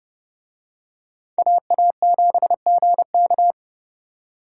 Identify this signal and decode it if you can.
Morse code, heard as a keyed tone, about 28 wpm, 700 Hz. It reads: AA7GK